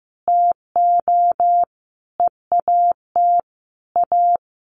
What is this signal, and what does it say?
Morse code, heard as a keyed tone, about 15 wpm, 700 Hz. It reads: TO EAT A
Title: Sentences from Top 600 Words 15wpm